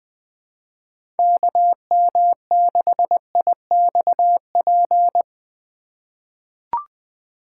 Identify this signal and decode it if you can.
Morse code, heard as a keyed tone, about 20 wpm, 700 Hz. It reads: KM6IXP
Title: USA Call Signs - 2x Character Spacing - 20wpm